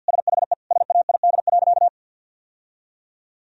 Morse code, heard as a keyed tone, about 50 wpm, 700 Hz. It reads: SHE SAID.
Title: Sentences from Top 1000 Words - Review Part 2 - 50wpm